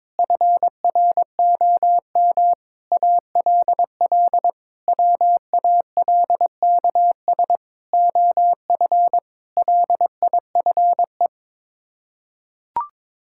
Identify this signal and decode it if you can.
Morse code, heard as a keyed tone, about 22 wpm, 700 Hz. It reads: FROM ALL WALKS OF LIFE
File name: practice / 22wpm / Sets of 6 Words - 22wpm